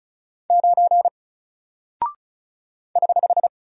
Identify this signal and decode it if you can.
Morse code, heard as a keyed tone, about 36 wpm, 700 Hz. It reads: 9 <HH>
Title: Single Character (Letter:Number:Punctuation:Prosign) Alt 35wpm